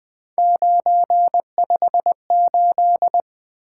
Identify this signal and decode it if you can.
Morse code, heard as a keyed tone, about 20 wpm, 700 Hz. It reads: 958
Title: Three Letter-Number Combinations - 20wpm